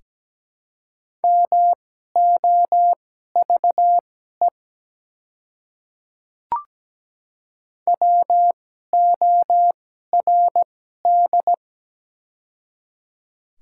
Morse code, heard as a keyed tone, about 17 wpm, spaced as for 12 wpm, 700 Hz. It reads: MOVE WORD
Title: 2x Character Spacing - Top 300 Words - 17wpm